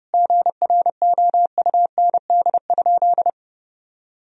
Morse code, heard as a keyed tone, about 30 wpm, 700 Hz. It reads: GROUND?
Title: Sentences from Top 1000 Words - Review Part 3 - 30wpm